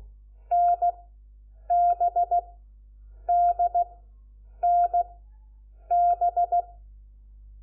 Morse code, heard as a keyed tone, about 16 wpm, 700 Hz. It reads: N B D N B
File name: Lecciones / CLASE Nº4